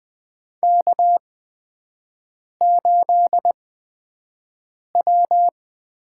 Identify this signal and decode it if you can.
Morse code, heard as a keyed tone, about 20 wpm, spaced as for 5 wpm, 700 Hz. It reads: K8W
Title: USA Call Signs - 5 Characters - 8x Character Spacing - 20wpm